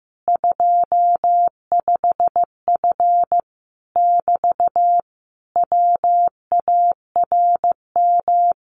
Morse code, heard as a keyed tone, about 15 wpm, 700 Hz. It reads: 25F <BT> WARM
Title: Common QSO Phrases Encore 15wpm